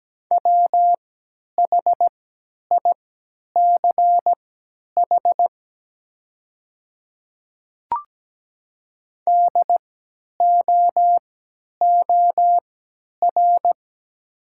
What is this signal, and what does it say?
Morse code, heard as a keyed tone, about 17 wpm, spaced as for 10 wpm, 700 Hz. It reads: WHICH DOOR
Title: 3x Character Spacing - Top 300 Words - 17wpm